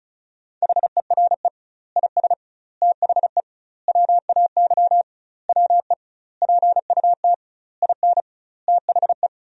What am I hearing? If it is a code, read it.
Morse code, heard as a keyed tone, about 34 wpm, 700 Hz. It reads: HERE IS THE WAY WE PUT IN THE
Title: 2x Word-Spacing - Sentences from Top 200 Words - 35wpm